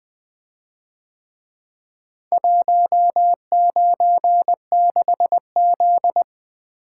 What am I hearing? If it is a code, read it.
Morse code, heard as a keyed tone, about 23 wpm, 700 Hz. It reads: 196Z
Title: Speed-Racing - Four Letter-Number Combinations - 20wpm